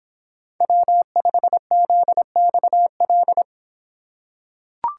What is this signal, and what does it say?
Morse code, heard as a keyed tone, about 26 wpm, 700 Hz. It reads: W5ZXL